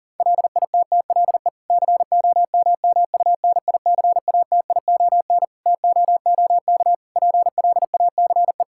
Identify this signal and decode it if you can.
Morse code, heard as a keyed tone, about 40 wpm, 700 Hz. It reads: LITTLE COMMUNICATION TOOK PLACE
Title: Sentences from Top 1000 Words - Review Part 6 - 40wpm